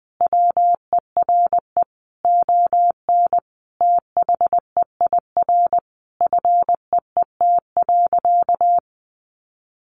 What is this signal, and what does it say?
Morse code, heard as a keyed tone, about 20 wpm, 700 Hz. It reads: WERE ON THEIR FEET.
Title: Sentences from Top 1000 Words 20wpm